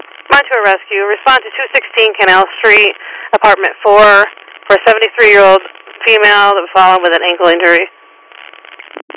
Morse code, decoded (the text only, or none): none